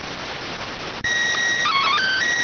SSTV (slow-scan television), PD 290.